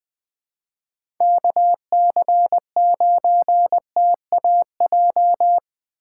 Morse code, heard as a keyed tone, about 20 wpm, 700 Hz. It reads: KC9TAJ
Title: USA Call Signs - 6 Characters - 2x Character Spacing - 20wpm